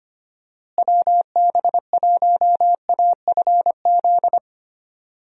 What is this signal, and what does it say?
Morse code, heard as a keyed tone, about 25 wpm, 700 Hz. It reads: WB1AFZ
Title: USA Call Signs - 6 Characters - Repeated 3x - 25wpm